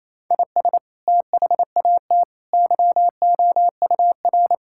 Morse code, heard as a keyed tone, about 28 wpm, 700 Hz. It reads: IS THAT YOUR
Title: Sentences from Top 1000 Words - Review Part 5 - 28wpm